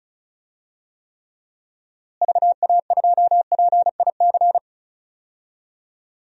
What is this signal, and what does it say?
Morse code, heard as a keyed tone, about 36 wpm, 700 Hz. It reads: VA2PIC
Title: DXCC-100 Call Signs - 6 Characters - Repeated 3x - 35wpm